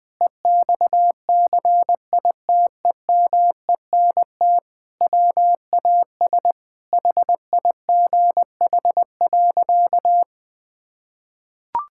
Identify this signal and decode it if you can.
Morse code, heard as a keyed tone, about 20 wpm, 700 Hz. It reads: EXCITEMENT WAS HIGH.
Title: Sentences with 5-Word Phrases - 20wpm